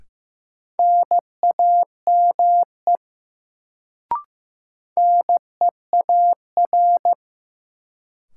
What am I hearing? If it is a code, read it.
Morse code, heard as a keyed tone, about 15 wpm, 700 Hz. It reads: NAME NEAR
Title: Sentences from Top 200 Words - New Word Drill - 15wpm